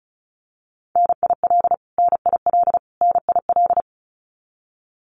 Morse code, heard as a keyed tone, about 34 wpm, 700 Hz. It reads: NIL NIL NIL
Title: Abbreviations Repeated 3x - 35wpm